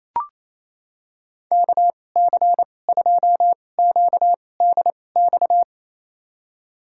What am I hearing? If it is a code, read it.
Morse code, heard as a keyed tone, about 28 wpm, spaced as for 20 wpm, 700 Hz. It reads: KC2QDX
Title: USA Call Signs - 6 Characters - 2x Character Spacing - 28wpm